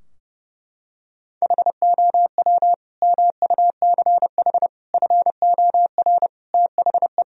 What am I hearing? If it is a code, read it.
Morse code, heard as a keyed tone, about 30 wpm, 700 Hz. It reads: HOW MUCH FOR THE
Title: Sentences from Top 1000 Words - Review Part 3 - 30wpm